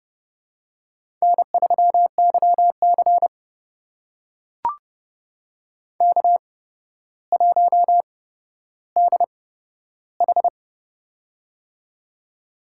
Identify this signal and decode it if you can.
Morse code, heard as a keyed tone, about 30 wpm, spaced as for 13 wpm, 700 Hz. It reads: N3YC K1DH
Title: USA Call Signs - 4 Characters - 8x Character Spacing - 30wpm